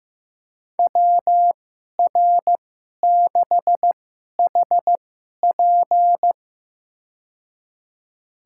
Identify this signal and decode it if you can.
Morse code, heard as a keyed tone, about 15 wpm, spaced as for 11 wpm, 700 Hz. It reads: WR6HP
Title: USA Call Signs - 5 Characters - 2x Character Spacing - 15wpm